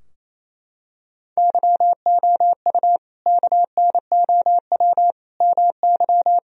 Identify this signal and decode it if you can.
Morse code, heard as a keyed tone, about 28 wpm, 700 Hz. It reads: YOU KNOW MY
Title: Sentences from Top 1000 Words - Review Part 3 - 28wpm